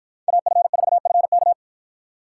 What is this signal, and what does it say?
Morse code, heard as a keyed tone, about 53 wpm, 700 Hz. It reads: AF4FK